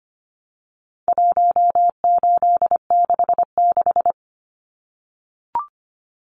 Morse code, heard as a keyed tone, about 25 wpm, 700 Hz. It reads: 1866
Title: Numbers - 4-Digit 25wpm